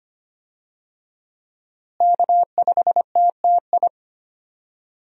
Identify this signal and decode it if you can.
Morse code, heard as a keyed tone, about 25 wpm, 700 Hz. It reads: K5TTI
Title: USA Call Signs - 5 Characters - Repeated 3x - 25wpm